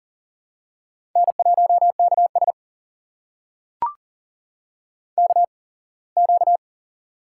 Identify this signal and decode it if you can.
Morse code, heard as a keyed tone, about 38 wpm, spaced as for 24 wpm, 700 Hz. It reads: N1KS KQ
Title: USA Call Signs - 4 Characters - 8x Character Spacing - 40wpm